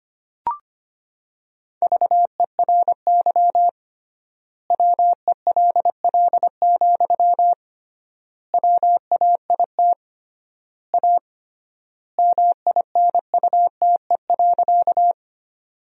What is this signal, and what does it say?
Morse code, heard as a keyed tone, about 25 wpm, 700 Hz. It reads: VERY WELL, WAIT A MINUTE.
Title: 3x Word-Spacing - Sentences from Top 500 Words - 25wpm